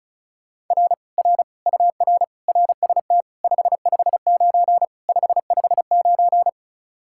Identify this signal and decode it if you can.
Morse code, heard as a keyed tone, about 44 wpm, 700 Hz. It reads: R R UR RST 559 559